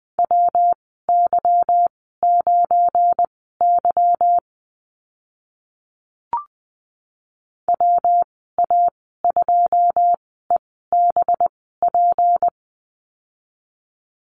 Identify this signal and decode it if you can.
Morse code, heard as a keyed tone, about 20 wpm, spaced as for 15 wpm, 700 Hz. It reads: WY9Y WA2EBP
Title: 186 Lesson 30g - Letter b - Call Signs 20wpm